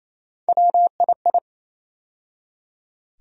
Morse code, heard as a keyed tone, about 28 wpm, 700 Hz. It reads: WII
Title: USA Call Sign Postfixes - 3 Letters 28wpm